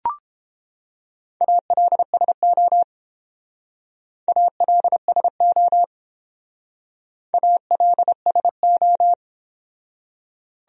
Morse code, heard as a keyed tone, about 29 wpm, 700 Hz. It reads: ALSO ALSO ALSO